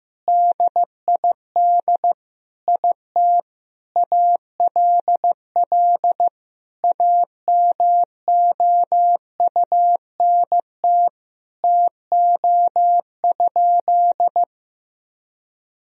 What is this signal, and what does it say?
Morse code, heard as a keyed tone, about 15 wpm, 700 Hz. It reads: DID IT ALL AMOUNT TO?
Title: Sentences from Top 1000 Words - Review Part 4 - 15wpm